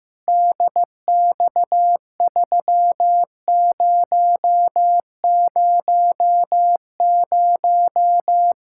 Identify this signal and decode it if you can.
Morse code, heard as a keyed tone, about 15 wpm, 700 Hz. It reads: DX3000
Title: Practice QSO - Exchange 2 - Info - 15wpm